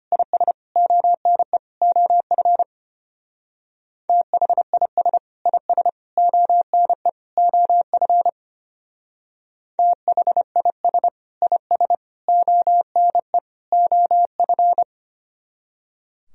Morse code, heard as a keyed tone, about 31 wpm, 700 Hz. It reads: IS ONE OF THIS IS ONE OF THIS IS ONE OF